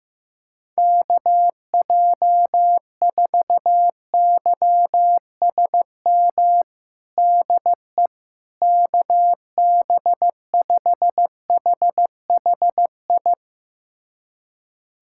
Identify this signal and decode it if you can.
Morse code, heard as a keyed tone, about 15 wpm, 700 Hz. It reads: KJ4YSM DE KB5HHI
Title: USA Callsign DE Callsign - 6 Characters - 15wpm